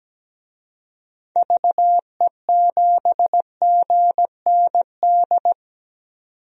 Morse code, heard as a keyed tone, about 17 wpm, 700 Hz. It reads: VE7GND
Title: DXCC-100 Call Signs - 6 Characters - 17wpm